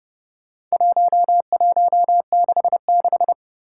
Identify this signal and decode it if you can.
Morse code, heard as a keyed tone, about 30 wpm, 700 Hz. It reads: 1166
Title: Four Challenging Letter-Number Combinations 30wpm